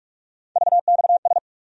Morse code, heard as a keyed tone, about 47 wpm, 700 Hz. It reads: VXS
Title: USA Call Sign Postfixes - 3 Letters 45wpm